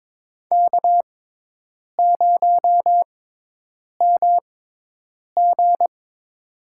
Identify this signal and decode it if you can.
Morse code, heard as a keyed tone, about 22 wpm, spaced as for 8 wpm, 700 Hz. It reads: K0MG
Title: USA Call Signs - 6x Character Spacing - 22wpm